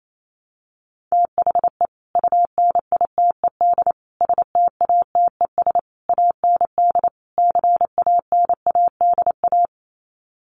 Morse code, heard as a keyed tone, about 28 wpm, 700 Hz. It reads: THE UNITED STATES AND CANADA
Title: Sets of 5 Words 28wpm